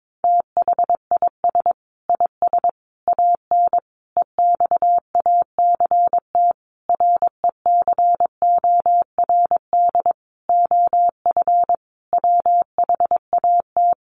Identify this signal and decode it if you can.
Morse code, heard as a keyed tone, about 22 wpm, 700 Hz. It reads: THIS IS AN EXACT RECORD OF WHAT
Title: Sentences from Top 1300 Words 22wpm